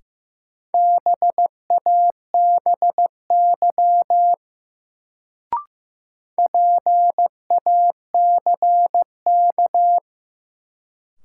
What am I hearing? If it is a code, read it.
Morse code, heard as a keyed tone, about 15 wpm, 700 Hz. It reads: BABY PACK
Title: Sentences from Top 1300 Words - New Word Drill - 15wpm